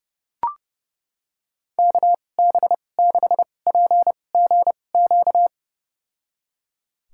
Morse code, heard as a keyed tone, about 30 wpm, spaced as for 22 wpm, 700 Hz. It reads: KB6PGQ
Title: USA Call Signs - 6 Characters - 2x Character Spacing - 30wpm